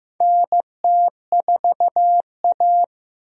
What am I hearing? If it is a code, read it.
Morse code, heard as a keyed tone, about 16 wpm, 700 Hz. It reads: NT4A
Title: USA Call Signs - 4 Characters - Repeated 2x - 15wpm